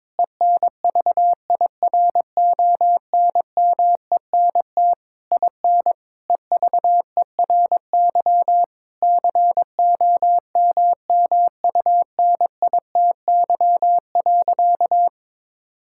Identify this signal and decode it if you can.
Morse code, heard as a keyed tone, about 21 wpm, 700 Hz. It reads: ENVIRONMENT IN EVERY COMMUNITY.
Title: Sentences from Top 1000 Words - Review Part 5 - 22wpm